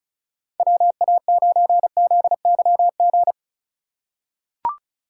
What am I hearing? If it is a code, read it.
Morse code, heard as a keyed tone, about 36 wpm, 700 Hz. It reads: WA9ZYG
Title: USA Call Signs - 2x Character Spacing - 35wpm